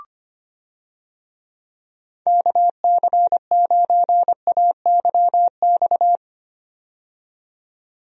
Morse code, heard as a keyed tone, about 25 wpm, 700 Hz. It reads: KC9AYX
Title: USA Call Signs - 6 Characters - 25wpm